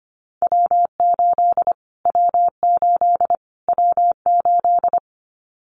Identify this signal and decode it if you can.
Morse code, heard as a keyed tone, about 25 wpm, 700 Hz. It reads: W8 W8 W8
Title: Callsign Prefixes for Top 25 DXCC - Repeated 3x - 25wpm